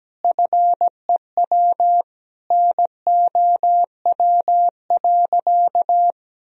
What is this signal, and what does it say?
Morse code, heard as a keyed tone, about 17 wpm, 700 Hz. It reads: FEW NOW.
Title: Sentences from Top 1000 Words - Review Part 1 - 17wpm